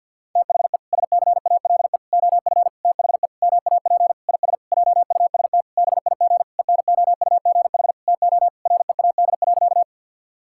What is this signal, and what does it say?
Morse code, heard as a keyed tone, about 49 wpm, 700 Hz. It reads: THE SCALE OF THE MAP IS JUST BIG ENOUGH TO READ.
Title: Sentences from Top 1000 Words - Review Part 5 - 50wpm